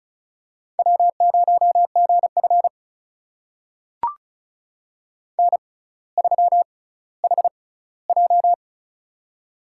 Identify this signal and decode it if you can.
Morse code, heard as a keyed tone, about 36 wpm, spaced as for 18 wpm, 700 Hz. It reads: W0GF N3HJ
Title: USA Call Signs - 4 Characters - 6x Character Spacing - 35wpm